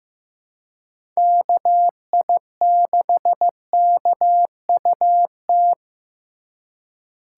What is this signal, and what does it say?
Morse code, heard as a keyed tone, about 15 wpm, 700 Hz. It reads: KI6KUT
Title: USA Call Signs - 6 Characters - 4x Character Spacing - 15wpm